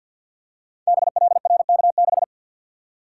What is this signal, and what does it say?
Morse code, heard as a keyed tone, about 47 wpm, 700 Hz. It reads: BLRK6
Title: Code Groups - 5 Letter:Numbers - 50wpm